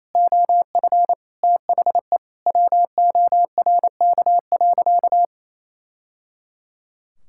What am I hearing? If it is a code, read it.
Morse code, heard as a keyed tone, about 28 wpm, 700 Hz. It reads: OF THE WORK.
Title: Sentences from Top 1000 Words - Review Part 6 - 28wpm